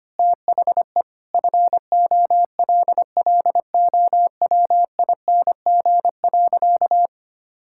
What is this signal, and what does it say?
Morse code, heard as a keyed tone, about 25 wpm, 700 Hz. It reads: THE FOLLOWING.
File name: practice / 25wpm / Sentences with 9-Word Phrases - 25wpm